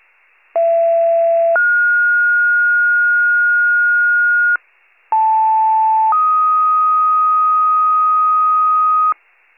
Two-tone page, heard as a one-second tone, then a three-second tone, about 670 then 1430 Hz, about 880 then 1250 Hz, with noise. It is heard from 0.6 to 9.1 s.
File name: pager / 39 RESCUE / wcalert